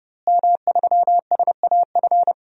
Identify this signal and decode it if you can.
Morse code, heard as a keyed tone, about 30 wpm, 700 Hz. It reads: M3SAF
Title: DXCC-100 Call Signs - 5 Characters - Repeated 3x - 30wpm